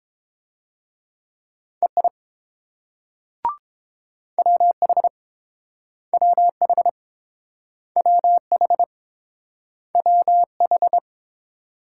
Morse code, heard as a keyed tone, about 30 wpm, 700 Hz. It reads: EI WH WH WH WH